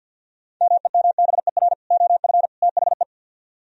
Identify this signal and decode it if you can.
Morse code, heard as a keyed tone, about 64 wpm, 700 Hz. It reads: MEMBER OF THE